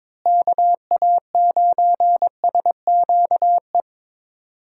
Morse code, heard as a keyed tone, about 21 wpm, 700 Hz. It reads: KA9SQE
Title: USA Call Signs - 6 Characters - Repeated 2x - 22wpm